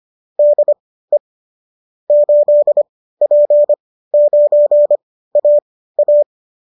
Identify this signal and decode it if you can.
Morse code, heard as a keyed tone, about 25 wpm, spaced as for 15 wpm, 590 Hz. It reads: DE 8P9AA